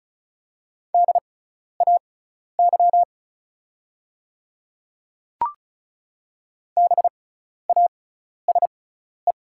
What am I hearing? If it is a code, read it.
Morse code, heard as a keyed tone, about 34 wpm, spaced as for 12 wpm, 690 Hz. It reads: DAY BASE
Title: 6x Character Spacing - Top 300 Words - 35wpm